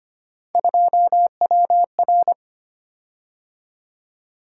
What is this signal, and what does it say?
Morse code, heard as a keyed tone, about 25 wpm, 700 Hz. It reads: 2WR
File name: Three Letter-Number Combinations - 25wpm